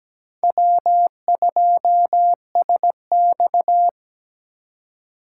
Morse code, heard as a keyed tone, about 17 wpm, 700 Hz. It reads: W2SX
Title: USA Call Signs - 4 Characters - 2x Character Spacing - 17wpm